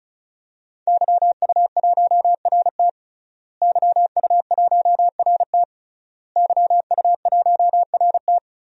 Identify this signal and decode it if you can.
Morse code, heard as a keyed tone, about 34 wpm, 700 Hz. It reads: YU1RT YU1RT YU1RT
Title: DXCC-100 Call Signs - 5 Characters - Repeated 3x - 35wpm